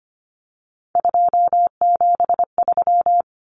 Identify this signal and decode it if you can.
Morse code, heard as a keyed tone, about 25 wpm, 700 Hz. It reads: 273
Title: Common Cut Numbers - 3-Digit 25wpm